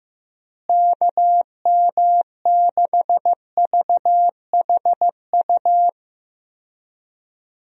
Morse code, heard as a keyed tone, about 15 wpm, 700 Hz. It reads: KM6VHU
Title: USA Call Signs - 6 Characters - Repeated 3x - 15wpm